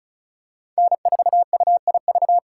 Morse code, heard as a keyed tone, about 36 wpm, 700 Hz. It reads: N4UIV
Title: USA Call Signs - 5 Characters - 2x Character Spacing - 35wpm